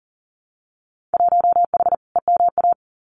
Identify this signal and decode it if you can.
Morse code, heard as a keyed tone, about 42 wpm, 700 Hz. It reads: 1H EMA
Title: Winter Field Day Exchanges 40wpm